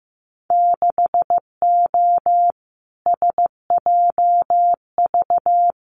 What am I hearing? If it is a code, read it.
Morse code, heard as a keyed tone, about 15 wpm, 700 Hz. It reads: 6O SJV
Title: Winter Field Day Exchanges Encore 15wpm